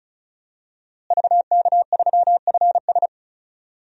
Morse code, heard as a keyed tone, about 34 wpm, 700 Hz. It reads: VK3FS
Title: USA Call Signs - 5 Characters - Repeated 2x - 35wpm